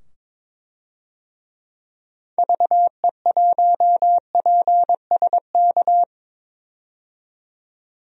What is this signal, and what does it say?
Morse code, heard as a keyed tone, about 22 wpm, 700 Hz. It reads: VE1PSK
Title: DXCC-100 Call Signs - 6 Characters - 22wpm